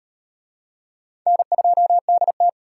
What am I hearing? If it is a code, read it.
Morse code, heard as a keyed tone, about 38 wpm, 700 Hz. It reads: N2DT